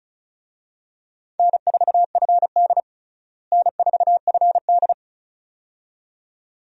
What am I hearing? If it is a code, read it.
Morse code, heard as a keyed tone, about 36 wpm, 700 Hz. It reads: N4FD N4FD
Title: USA Call Signs - 4 Characters - Repeated 2x - 35wpm